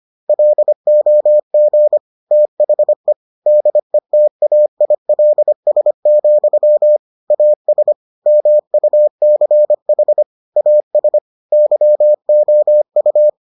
Morse code, heard as a keyed tone, about 25 wpm, 600 Hz. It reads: LOG THE DETAILS, AS MUCH AS YOU